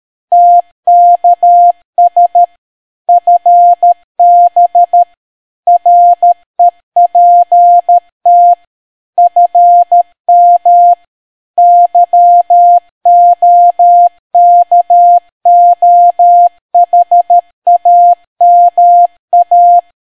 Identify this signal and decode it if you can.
Morse code, heard as a keyed tone, about 13 wpm, 700 Hz. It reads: TKS FB REPT FM YOKOHAMA